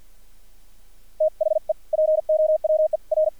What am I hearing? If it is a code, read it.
Morse code, heard as a keyed tone, about 50 wpm, spaced as for 42 wpm, 620 Hz. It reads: THE POWER
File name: archived / CRSnet